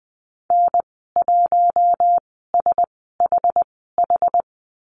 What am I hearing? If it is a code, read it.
Morse code, heard as a keyed tone, about 20 wpm, spaced as for 14 wpm, 700 Hz. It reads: N1SHH
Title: 082 Lesson 14g - Number 2 - Call Signs 20wpm